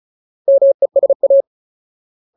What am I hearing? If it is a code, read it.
Morse code, heard as a keyed tone, about 34 wpm, 550 Hz. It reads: MESA
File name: Ultimate Word List / 35WPM Ultimate